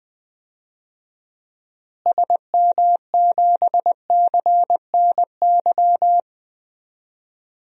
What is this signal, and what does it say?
Morse code, heard as a keyed tone, about 21 wpm, 700 Hz. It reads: SM7CNY
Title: DXCC-100 Call Signs - 6 Characters - Repeated 3x - 20wpm